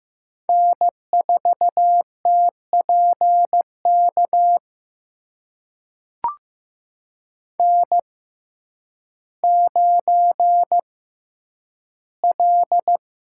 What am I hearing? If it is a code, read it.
Morse code, heard as a keyed tone, about 16 wpm, spaced as for 9 wpm, 700 Hz. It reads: N4TPK N9L
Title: USA Call Signs - 5 Characters - 6x Character Spacing - 15wpm